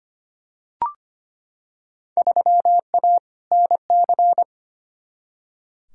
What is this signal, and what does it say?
Morse code, heard as a keyed tone, about 25 wpm, 700 Hz. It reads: 3A NC
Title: Field Day Exchanges - 25wpm